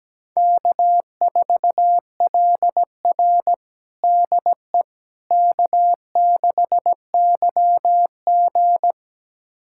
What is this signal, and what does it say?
Morse code, heard as a keyed tone, about 17 wpm, 700 Hz. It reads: K4LR DE K6YG
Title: USA Callsign DE Callsign - 4 Characters - 17wpm